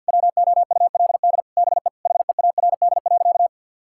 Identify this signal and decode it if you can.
Morse code, heard as a keyed tone, about 50 wpm, 700 Hz. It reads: WOULD BE HEARD.
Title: Sentences from Top 1600 Words 50wpm